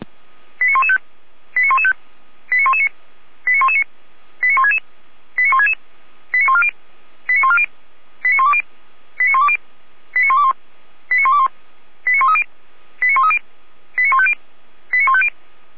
Five-tone selective call (ZVEI), heard as quick runs of five tones, tones of about 70 ms. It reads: 89115 89115 89119 89119 89166 89166 89140 89140 89120 89120 89121 89121 89130 89130 89150 89150